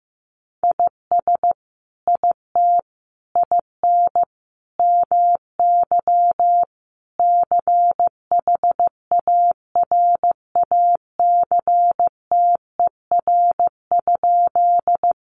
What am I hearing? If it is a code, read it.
Morse code, heard as a keyed tone, about 15 wpm, 700 Hz. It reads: IS IT IN MY CHARACTER?
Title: Sentences from Top 1200 Words 15wpm